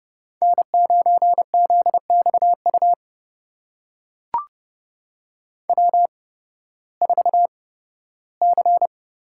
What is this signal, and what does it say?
Morse code, heard as a keyed tone, about 31 wpm, spaced as for 16 wpm, 700 Hz. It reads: N9ZXU W4C
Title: USA Call Signs - 5 Characters - 8x Character Spacing - 30wpm